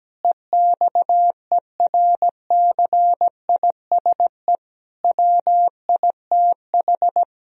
Morse code, heard as a keyed tone, about 17 wpm, 700 Hz. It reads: EXERCISE WITH
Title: Sentences from Top 1000 Words - Review Part 4 - 17wpm